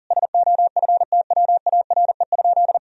Morse code, heard as a keyed tone, about 40 wpm, 700 Hz. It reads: SOFTWARE?
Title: Sentences from Top 1000 Words - Review Part 3 - 40wpm